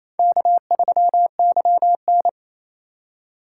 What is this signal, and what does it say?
Morse code, heard as a keyed tone, about 28 wpm, 700 Hz. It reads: K3YN
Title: USA Call Signs - 4 Characters - 3x Character Spacing - 28wpm